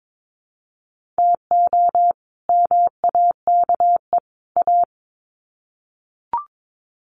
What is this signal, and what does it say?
Morse code, heard as a keyed tone, about 22 wpm, 700 Hz. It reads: TO MAKE A
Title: Sets of 3 Words 22wpm